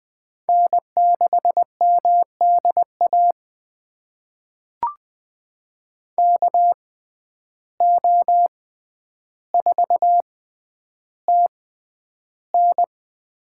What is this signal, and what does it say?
Morse code, heard as a keyed tone, about 20 wpm, spaced as for 10 wpm, 700 Hz. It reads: N6MDA KO4TN
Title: USA Call Signs - 6x Character Spacing - 20wpm